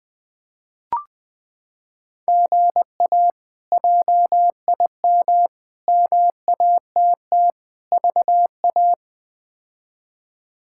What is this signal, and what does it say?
Morse code, heard as a keyed tone, about 20 wpm, 700 Hz. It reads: GA JIM MATT VA
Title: K1USN SST QSO Practice - 20wpm